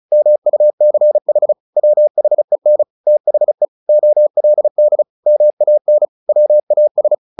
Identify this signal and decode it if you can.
Morse code, heard as a keyed tone, about 36 wpm, 600 Hz. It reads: MUCH WHEN THE OLD MAN WAS
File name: the old man and the sea / the old man and the sea 1 35wpm